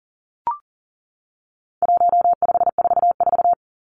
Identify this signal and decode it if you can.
Morse code, heard as a keyed tone, about 40 wpm, 700 Hz. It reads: 1544
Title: Numbers - 4-Digit 40wpm